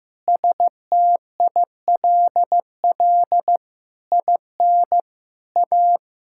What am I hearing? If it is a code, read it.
Morse code, heard as a keyed tone, about 14 wpm, 700 Hz. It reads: STILL IN A
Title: Sentences from Top 1000 Words - Review Part 5 - 15wpm